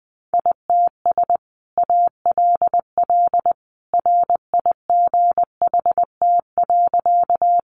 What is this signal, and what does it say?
Morse code, heard as a keyed tone, about 19 wpm, 700 Hz. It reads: ITS ALL RIGHT.
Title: Sentences from Top 600 Words 20wpm